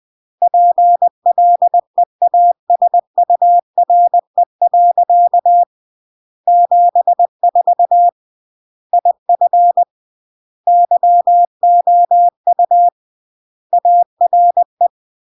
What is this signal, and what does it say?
Morse code, heard as a keyed tone, about 20 wpm, 700 Hz. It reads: PLEASURE. 74 IF YOU ARE
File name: CW / jodorowsky / 82 commandments 20-20